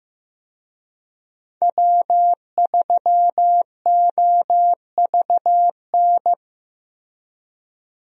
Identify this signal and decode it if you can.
Morse code, heard as a keyed tone, about 15 wpm, 700 Hz. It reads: W3OVN
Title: USA Call Signs - 5 Characters - Repeated 3x - 15wpm